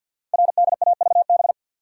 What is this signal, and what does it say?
Morse code, heard as a keyed tone, about 51 wpm, 700 Hz. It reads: RDAVB
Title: Code Groups - 5 Letters - 50wpm